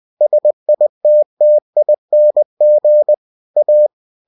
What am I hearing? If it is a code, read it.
Morse code, heard as a keyed tone, about 20 wpm, 600 Hz. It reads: SITTING A
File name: cw-books / winnie-the-pooh / winnie-20wpm-page-0057